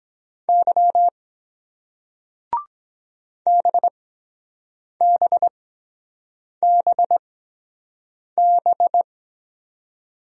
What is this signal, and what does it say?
Morse code, heard as a keyed tone, about 23 wpm, spaced as for 7 wpm, 700 Hz. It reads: Y BBBB